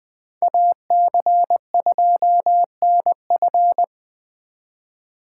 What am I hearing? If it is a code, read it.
Morse code, heard as a keyed tone, about 20 wpm, 700 Hz. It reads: AC2NF
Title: Speed Racing - USA 2x2 Call Signs 17wpm